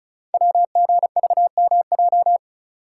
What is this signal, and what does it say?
Morse code, heard as a keyed tone, about 35 wpm, 690 Hz. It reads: WGVMJ
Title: Code Groups - 5 Letters - 35wpm